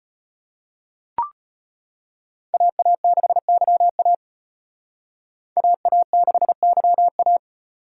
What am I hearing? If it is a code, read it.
Morse code, heard as a keyed tone, about 31 wpm, 700 Hz. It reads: AA6YA AA6YA